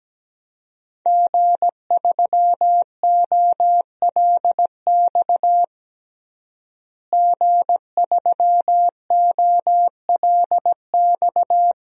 Morse code, heard as a keyed tone, about 17 wpm, 700 Hz. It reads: G3OLX G3OLX
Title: DXCC-100 Call Signs - 5 Characters - Repeated 3x - 17wpm